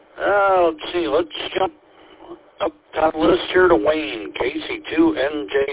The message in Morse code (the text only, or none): none